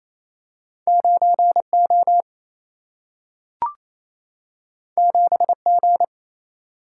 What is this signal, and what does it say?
Morse code, heard as a keyed tone, about 28 wpm, 700 Hz. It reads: 9 O 7 G
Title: Single-Letter Single-Number 28wpm